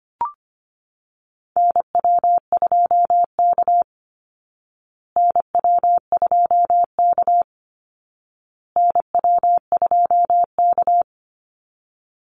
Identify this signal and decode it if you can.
Morse code, heard as a keyed tone, about 25 wpm, 700 Hz. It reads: NW2K NW2K NW2K
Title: Call Signs Repeated 2x - 25wpm